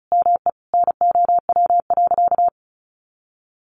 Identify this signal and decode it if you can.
Morse code, heard as a keyed tone, about 34 wpm, 700 Hz. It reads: ME NOW.
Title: Sentences from Top 1300 Words 35wpm